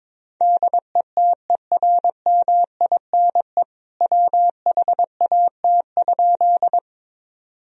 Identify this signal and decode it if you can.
Morse code, heard as a keyed tone, about 22 wpm, 700 Hz. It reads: DETERMINE WHAT?
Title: Sentences from Top 1000 Words - Review Part 4 - 22wpm